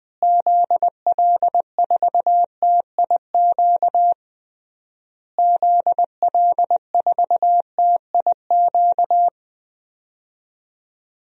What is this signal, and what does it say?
Morse code, heard as a keyed tone, about 21 wpm, 700 Hz. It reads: ZL4TIQ ZL4TIQ
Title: DXCC-100 Call Signs - 6 Characters - Repeated 3x - 20wpm